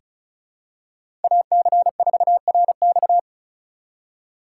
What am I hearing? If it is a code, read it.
Morse code, heard as a keyed tone, about 36 wpm, 700 Hz. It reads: AC4RX
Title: USA Call Signs - 5 Characters - 4x Character Spacing - 35wpm